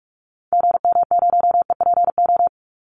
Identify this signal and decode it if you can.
Morse code, heard as a keyed tone, about 44 wpm, 700 Hz. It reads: GM0EPO
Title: Call Signs Repeated 2x - 45wpm